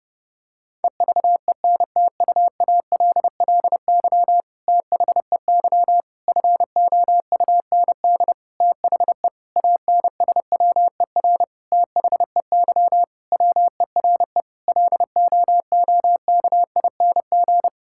Morse code, heard as a keyed tone, about 32 wpm, 700 Hz. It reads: EVENTUALLY THEY FOUND THE ANSWER THEY WERE LOOKING
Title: Sentences from Top 1000 Words - Review Part 6 - 30wpm